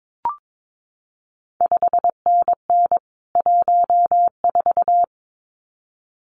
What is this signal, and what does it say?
Morse code, heard as a keyed tone, about 22 wpm, 700 Hz. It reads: 5NN 14
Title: CQ WW DX Contest Exchange - 22wpm